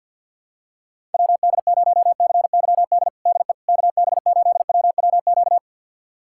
Morse code, heard as a keyed tone, about 51 wpm, 700 Hz. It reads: WD0CYD DE KB8WWQ
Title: USA Callsign DE Callsign - 6 Characters - 50wpm